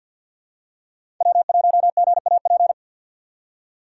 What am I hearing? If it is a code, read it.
Morse code, heard as a keyed tone, about 52 wpm, 700 Hz. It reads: W1GAP
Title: USA Call Signs - 5 Characters - 2x Character Spacing - 50wpm